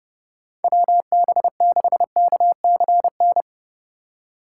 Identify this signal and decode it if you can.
Morse code, heard as a keyed tone, about 32 wpm, 700 Hz. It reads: WB6KCN
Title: USA Call Signs - 6 Characters - 3x Character Spacing - 30wpm